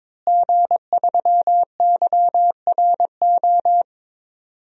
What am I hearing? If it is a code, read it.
Morse code, heard as a keyed tone, about 22 wpm, 700 Hz. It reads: G3YRO
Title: DXCC-100 Call Signs - 5 Characters - Repeated 3x - 22wpm